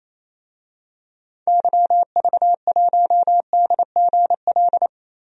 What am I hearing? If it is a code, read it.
Morse code, heard as a keyed tone, about 28 wpm, 700 Hz. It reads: YV1DGL
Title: DXCC-100 Call Signs - 6 Characters - Repeated 3x - 28wpm